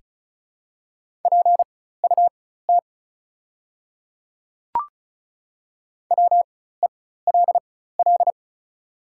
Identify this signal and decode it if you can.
Morse code, heard as a keyed tone, about 34 wpm, spaced as for 16 wpm, 700 Hz. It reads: PUT WELL
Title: 4x Character Spacing - Top 200 Words - 35wpm